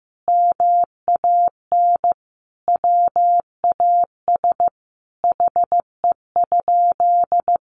Morse code, heard as a keyed tone, about 15 wpm, 700 Hz. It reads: MAN WAS HE?
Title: Sentences from Top 1300 Words 15wpm